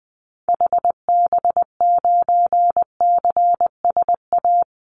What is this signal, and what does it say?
Morse code, heard as a keyed tone, about 20 wpm, 700 Hz. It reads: HB9CSA
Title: Call Signs - 20wpm